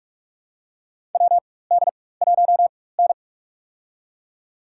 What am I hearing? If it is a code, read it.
Morse code, heard as a keyed tone, about 44 wpm, spaced as for 21 wpm, 700 Hz. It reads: WD1N
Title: USA Call Signs - 4 Characters - 4x Character Spacing - 45wpm